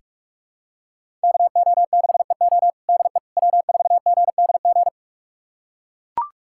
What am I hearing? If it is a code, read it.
Morse code, heard as a keyed tone, about 44 wpm, 700 Hz. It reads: KO6EO DE W4GDG
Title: USA Callsign DE Callsign - 5 Characters - 45wpm